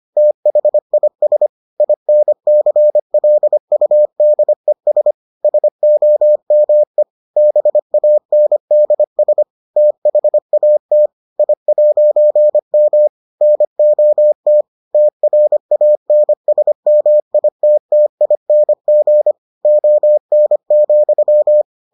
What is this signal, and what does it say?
Morse code, heard as a keyed tone, about 25 wpm, 600 Hz. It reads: THIS INCLUDES SOME BANDS THAT I'M NOT TRANSMITTING ON,